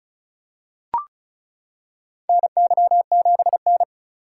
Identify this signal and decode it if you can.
Morse code, heard as a keyed tone, about 36 wpm, 700 Hz. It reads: NY7N
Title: K1USN SST QSO Practice - 35wpm